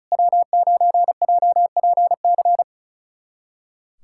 Morse code, heard as a keyed tone, about 36 wpm, 700 Hz. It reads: W9JPC
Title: USA Callsign DE Callsign - 5 Characters - 35wpm